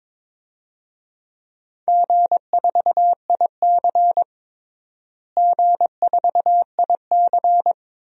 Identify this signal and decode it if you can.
Morse code, heard as a keyed tone, about 21 wpm, 700 Hz. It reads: G4IC G4IC
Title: DXCC-100 Call Signs - 4 Characters - Repeated 2x - 22wpm